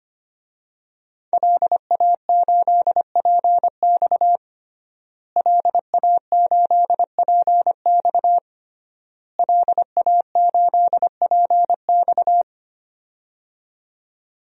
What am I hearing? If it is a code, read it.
Morse code, heard as a keyed tone, about 25 wpm, 700 Hz. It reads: LA8PX LA8PX LA8PX
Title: DXCC-100 Call Signs - 5 Characters - Repeated 3x - 25wpm